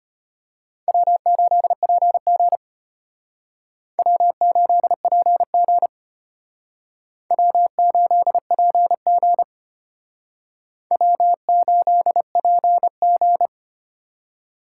Morse code, heard as a keyed tone, about 32 wpm, 700 Hz. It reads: W8PG W8PG W8PG W8PG